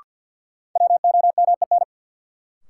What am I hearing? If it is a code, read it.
Morse code, heard as a keyed tone, about 53 wpm, 700 Hz. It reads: WOMEN
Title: Sentences from Top 1300 Words - New Word Drill Alt - 50wpm